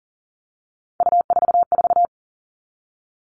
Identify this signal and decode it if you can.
Morse code, heard as a keyed tone, about 40 wpm, 700 Hz. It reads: U44
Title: All Cut Numbers - 3-Digit 40wpm